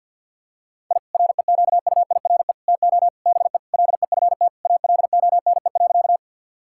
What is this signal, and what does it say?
Morse code, heard as a keyed tone, about 51 wpm, 700 Hz. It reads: I REQUIRE TO BE LEFT ALONE.
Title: Sentences from Top 1000 Words - Review Part 6 - 50wpm